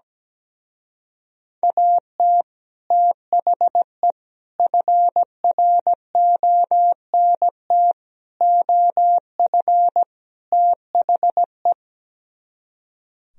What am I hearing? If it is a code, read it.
Morse code, heard as a keyed tone, about 17 wpm, 700 Hz. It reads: AT THE FRONT OF THE
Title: Speed-Racing - Sets of 5 Words 17wpm